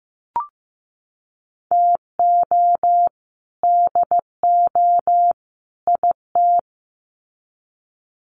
Morse code, heard as a keyed tone, about 15 wpm, 700 Hz. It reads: TO DO IT
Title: Sets of 3 Words 15wpm